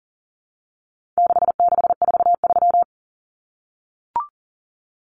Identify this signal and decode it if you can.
Morse code, heard as a keyed tone, about 40 wpm, 700 Hz. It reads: 6643
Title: Common Cut Numbers - 4-Digit 40wpm